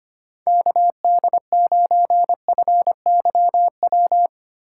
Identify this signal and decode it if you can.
Morse code, heard as a keyed tone, about 25 wpm, 700 Hz. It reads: KD9FYW
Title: USA Call Signs - 6 Characters - 8x Character Spacing - 25wpm